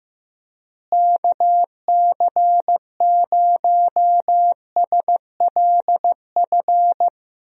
Morse code, heard as a keyed tone, about 15 wpm, 700 Hz. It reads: KC0SLF
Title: USA Call Signs - 6 Characters - Repeated 3x - 15wpm